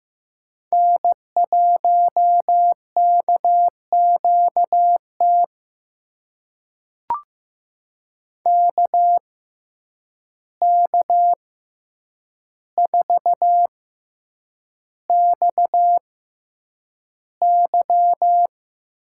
Morse code, heard as a keyed tone, about 15 wpm, spaced as for 8 wpm, 700 Hz. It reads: N1KQT KK4XY
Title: USA Call Signs - 5 Characters - 6x Character Spacing - 15wpm